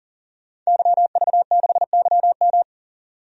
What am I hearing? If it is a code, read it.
Morse code, heard as a keyed tone, about 40 wpm, 700 Hz. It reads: YV6YM
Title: DXCC-100 Call Signs - 5 Characters - 40wpm